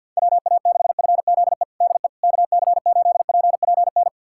Morse code, heard as a keyed tone, about 51 wpm, 700 Hz. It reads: WA6FZE DE KC8PPN
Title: USA Callsign DE Callsign - 6 Characters - 50wpm